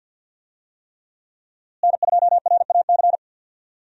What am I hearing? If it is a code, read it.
Morse code, heard as a keyed tone, about 52 wpm, 700 Hz. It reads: N2RAC
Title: USA Call Signs - 5 Characters - Repeated 3x - 50wpm